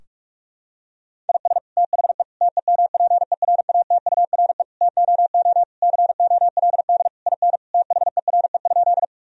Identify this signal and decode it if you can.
Morse code, heard as a keyed tone, about 45 wpm, 700 Hz. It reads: IS THE TEMPERATURE TOO COLD IN THERE?
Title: Sentences from Top 1000 Words - Review Part 4 - 45wpm